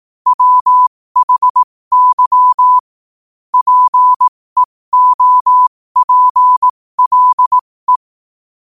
Morse code, heard as a keyed tone, about 18 wpm, spaced as for 15 wpm, 1000 Hz. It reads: WHY PEOPLE
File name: qotdfiles / 15wpm